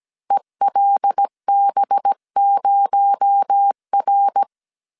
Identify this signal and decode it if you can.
Morse code, heard as a keyed tone, about 17 wpm, 800 Hz. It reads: ELB0R
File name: CW / DET / 1 6 1500C S C 16wpm 2733